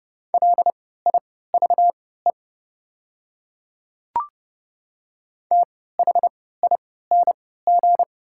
Morse code, heard as a keyed tone, about 30 wpm, spaced as for 17 wpm, 700 Hz. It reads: LIVE THING
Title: 3x Character Spacing - Top 200 Words - 30wpm